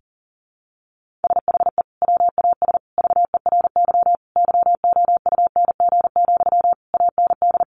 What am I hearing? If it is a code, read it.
Morse code, heard as a keyed tone, about 37 wpm, 700 Hz. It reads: SHE WAS VERY YOUNG, AND
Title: Sentences from Top 1400 Words 40wpm